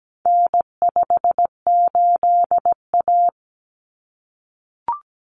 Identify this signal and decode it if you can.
Morse code, heard as a keyed tone, about 17 wpm, 700 Hz. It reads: N58A
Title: Common Cut Numbers - 4-Digit 17wpm